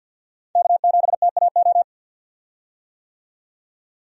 Morse code, heard as a keyed tone, about 50 wpm, 700 Hz. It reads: K7TAO
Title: USA Call Signs - 5 Characters - 50wpm